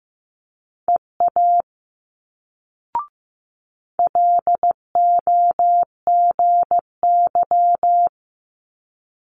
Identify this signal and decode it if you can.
Morse code, heard as a keyed tone, about 15 wpm, 700 Hz. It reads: EA LOGY
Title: Multi-letter Phonemes Quick Fire 15wpm